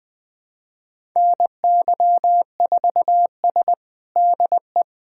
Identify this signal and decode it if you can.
Morse code, heard as a keyed tone, about 20 wpm, 700 Hz. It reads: NY4S DE
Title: USA Callsign DE Callsign - 4 Characters - 20wpm